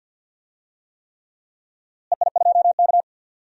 Morse code, heard as a keyed tone, about 50 wpm, 700 Hz. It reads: EI2K
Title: DXCC-100 Call Signs - 4 Characters - Repeated 3x - 50wpm